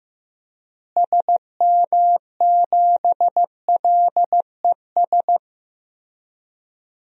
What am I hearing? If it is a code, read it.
Morse code, heard as a keyed tone, about 15 wpm, 700 Hz. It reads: SM7LES
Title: DXCC-100 Call Signs - 6 Characters - Repeated 2x - 15wpm